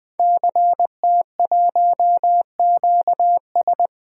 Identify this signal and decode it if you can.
Morse code, heard as a keyed tone, about 20 wpm, 700 Hz. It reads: CT1QS
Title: DXCC-100 Call Signs - 5 Characters - 20wpm